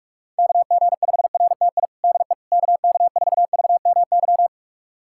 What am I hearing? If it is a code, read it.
Morse code, heard as a keyed tone, about 44 wpm, 700 Hz. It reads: KG5RTI DE KK4VMY
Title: USA Callsign DE Callsign 45wpm